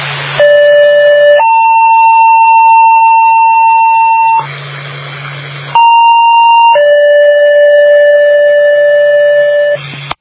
Two-tone page, heard as a one-second tone, then a three-second tone, about 580 then 900 Hz, about 930 then 580 Hz. It is heard from 0.4 to 9.8 s.